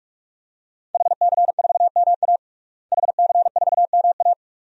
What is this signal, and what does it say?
Morse code, heard as a keyed tone, about 46 wpm, 700 Hz. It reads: HC4MA HC4MA
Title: DXCC-100 Call Signs - 5 Characters - Repeated 2x - 45wpm